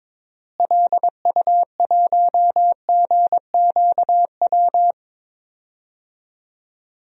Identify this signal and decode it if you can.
Morse code, heard as a keyed tone, about 21 wpm, 700 Hz. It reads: LU1GQW
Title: DXCC-100 Call Signs - 6 Characters - Repeated 2x - 22wpm